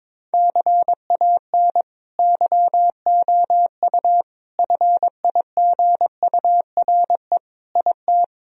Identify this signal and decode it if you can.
Morse code, heard as a keyed tone, about 21 wpm, 700 Hz. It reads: CAN YOU FIGURE IT
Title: Sentences from Top 1000 Words - Review Part 2 - 22wpm